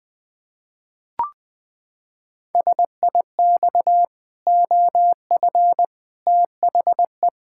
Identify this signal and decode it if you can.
Morse code, heard as a keyed tone, about 19 wpm, 700 Hz. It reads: SIX OF THE
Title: Pangram Challenge v2 - 20wpm